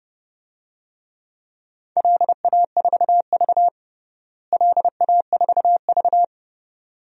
Morse code, heard as a keyed tone, about 32 wpm, 700 Hz. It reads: LA4V LA4V
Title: DXCC-100 Call Signs - 4 Characters - Repeated 2x - 30wpm